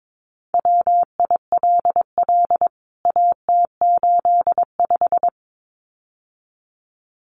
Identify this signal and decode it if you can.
Morse code, heard as a keyed tone, about 22 wpm, 700 Hz. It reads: WILL AT85
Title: CWT Exchange 22wpm